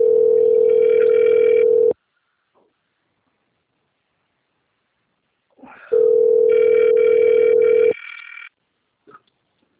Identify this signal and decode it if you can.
Ringback tone (North America), 440 + 480 Hz.
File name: out- - - - - .47634